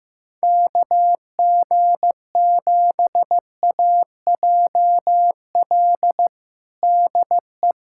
Morse code, heard as a keyed tone, about 15 wpm, 700 Hz. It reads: KG7AJL DE
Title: USA Callsign DE Callsign - 6 Characters - 15wpm